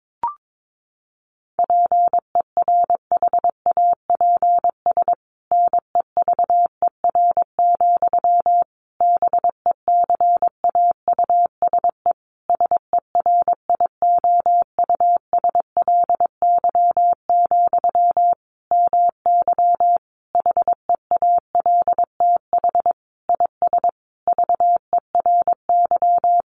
Morse code, heard as a keyed tone, about 22 wpm, 700 Hz. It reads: PERHAPS NEVER, BECAUSE SERIOUSLY, MY HEALTH IS VERY
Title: Sentences from Top 1200 Words 22wpm